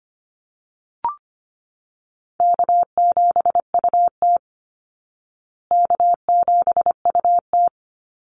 Morse code, heard as a keyed tone, about 25 wpm, 700 Hz. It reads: K7UT K7UT
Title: Easy Call Signs Repeated 2x - 25wpm